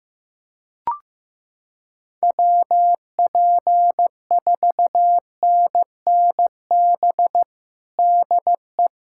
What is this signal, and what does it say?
Morse code, heard as a keyed tone, about 15 wpm, 700 Hz. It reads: WP4NNB DE
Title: USA Callsign DE Callsign - 6 Characters - 15wpm